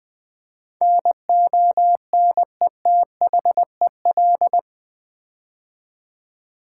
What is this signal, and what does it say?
Morse code, heard as a keyed tone, about 20 wpm, 700 Hz. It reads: NONETHEL
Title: Build A Word - 4 Letters At A Time - 11-Letter Words - 20wpm